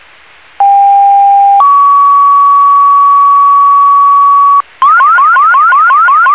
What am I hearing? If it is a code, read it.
Two-tone page, heard as a one-second tone, then a three-second tone, about 800 then 1130 Hz, with noise. It is heard from 0.6 to 4.6 s.